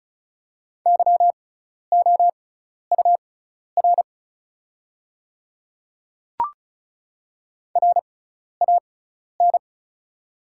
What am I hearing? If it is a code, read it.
Morse code, heard as a keyed tone, about 36 wpm, spaced as for 12 wpm, 700 Hz. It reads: YOUR RAN
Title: 6x Character Spacing - Top 500 Words - 35wpm